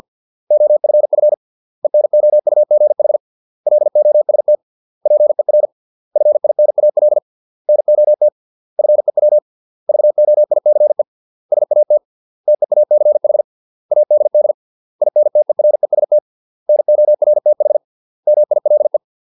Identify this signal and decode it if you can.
Morse code, heard as a keyed tone, about 50 wpm, 600 Hz. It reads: OFF ENOUGH LOST PER FINAL NOT FEW VOICE SAT TEACH ADD INTEREST NORTH MILE